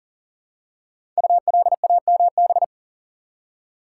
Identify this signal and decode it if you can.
Morse code, heard as a keyed tone, about 38 wpm, 700 Hz. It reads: ULAMB
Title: Code Groups - 5 Letters - 40wpm